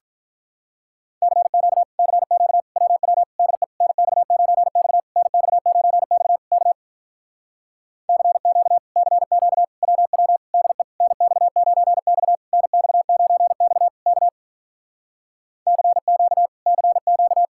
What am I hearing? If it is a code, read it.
Morse code, heard as a keyed tone, about 48 wpm, 700 Hz. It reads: CQ CQ WW DE NX9X NX9X K CQ CQ WW DE NX9X NX9X K CQ CQ